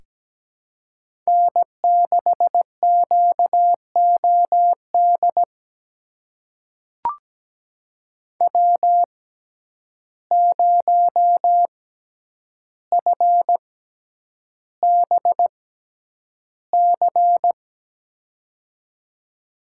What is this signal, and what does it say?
Morse code, heard as a keyed tone, about 17 wpm, spaced as for 9 wpm, 700 Hz. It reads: N6QOD W0FBC
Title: USA Call Signs - 5 Characters - 6x Character Spacing - 17wpm